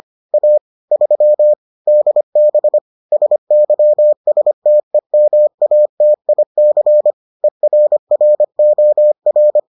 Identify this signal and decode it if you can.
Morse code, heard as a keyed tone, about 25 wpm, 600 Hz. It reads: A 3 DB SYSTEMATIC ERROR